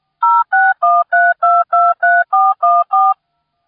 DTMF, heard as telephone touch tones, tones of about 220 ms. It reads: 0613223414